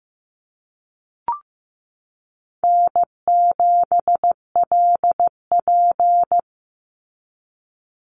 Morse code, heard as a keyed tone, about 15 wpm, 700 Hz. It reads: N7LP
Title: NA SOTA Activator Callsigns 15wpm